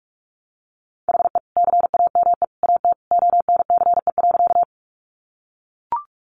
Morse code, heard as a keyed tone, about 44 wpm, 700 Hz. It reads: HE CAME AT ONCE.
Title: Sentences from Top 300 Words 45wpm